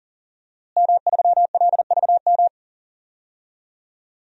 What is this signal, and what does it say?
Morse code, heard as a keyed tone, about 40 wpm, 700 Hz. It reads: M3LVM
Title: DXCC-100 Call Signs - 5 Characters - Repeated 2x - 40wpm